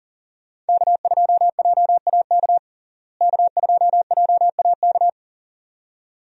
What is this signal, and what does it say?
Morse code, heard as a keyed tone, about 40 wpm, 700 Hz. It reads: K2JAK K2JAK
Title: USA Call Signs - 5 Characters - Repeated 2x - 40wpm